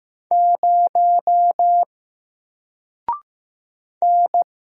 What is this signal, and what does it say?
Morse code, heard as a keyed tone, about 15 wpm, 700 Hz. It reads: 0 N
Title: Single Character (Letter:Number:Punctuation:Prosign) Alt 15wpm